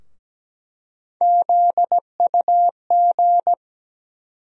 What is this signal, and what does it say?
Morse code, heard as a keyed tone, about 17 wpm, 700 Hz. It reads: ZUG
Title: USA Call Sign Postfixes - 3 Letters 17wpm